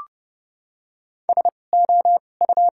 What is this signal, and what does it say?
Morse code, heard as a keyed tone, about 30 wpm, spaced as for 22 wpm, 700 Hz. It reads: SO U
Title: 2x Character Spacing - Top 100 Words - 30wpm